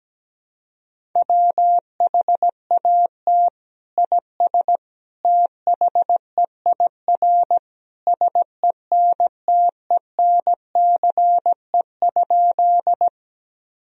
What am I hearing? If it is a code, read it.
Morse code, heard as a keyed tone, about 17 wpm, 700 Hz. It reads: WHAT IS THEIR SENTENCE?
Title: Sentences from Top 1000 Words - Review Part 1 - 17wpm